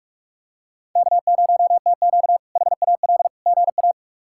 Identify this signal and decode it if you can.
Morse code, heard as a keyed tone, about 42 wpm, 700 Hz. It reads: K0TQ HAL GA
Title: K1USN SST QSO Practice - 45wpm